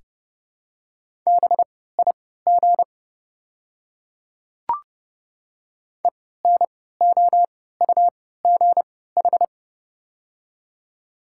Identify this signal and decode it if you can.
Morse code, heard as a keyed tone, about 32 wpm, spaced as for 17 wpm, 700 Hz. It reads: BIG ENOUGH
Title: 3x Character Spacing - Top 500 Words - 30wpm